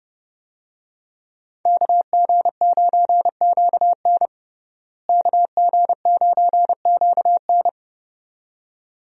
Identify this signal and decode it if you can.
Morse code, heard as a keyed tone, about 30 wpm, 700 Hz. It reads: KG9QN KG9QN
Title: USA Call Signs - 5 Characters - Repeated 2x - 30wpm